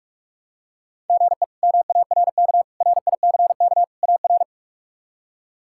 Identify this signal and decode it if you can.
Morse code, heard as a keyed tone, about 46 wpm, 700 Hz. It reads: GE MARK RICK AR
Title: K1USN SST QSO Practice - 45wpm